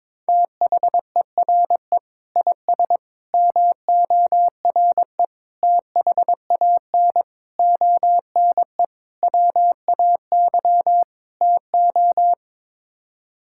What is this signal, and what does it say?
Morse code, heard as a keyed tone, about 22 wpm, 700 Hz. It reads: THERE IS MORE THAN ONE WAY TO
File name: practice / 22wpm / Sets of 7 Words - 22wpm